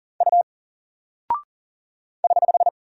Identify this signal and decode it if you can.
Morse code, heard as a keyed tone, about 40 wpm, 700 Hz. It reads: U <HH>
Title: Single Character (Letter:Number:Punctuation:Prosign) Alt 40wpm